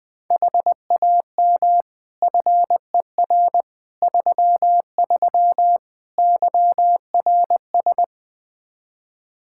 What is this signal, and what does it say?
Morse code, heard as a keyed tone, about 20 wpm, 700 Hz. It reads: HAM FER 33 YRS
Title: Practice QSO - Exchange 3 - More Info - 20wpm